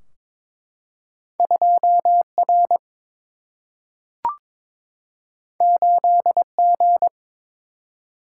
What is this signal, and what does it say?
Morse code, heard as a keyed tone, about 22 wpm, 700 Hz. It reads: 2 R 8 G
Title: Single-Letter Single-Number 22wpm